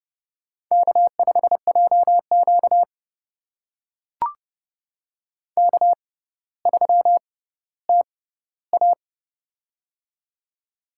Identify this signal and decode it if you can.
Morse code, heard as a keyed tone, about 30 wpm, spaced as for 15 wpm, 700 Hz. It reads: K5JQ K3TA
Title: USA Call Signs - 4 Characters - 6x Character Spacing - 30wpm